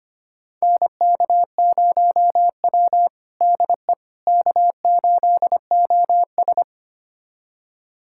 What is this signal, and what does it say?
Morse code, heard as a keyed tone, about 25 wpm, 700 Hz. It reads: NK0W DE K8OS
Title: USA Callsign DE Callsign - 4 Characters - 25wpm